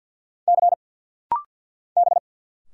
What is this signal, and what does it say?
Morse code, heard as a keyed tone, about 50 wpm, 690 Hz. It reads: C B
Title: Single Letters - Rapid-Fire 50wpm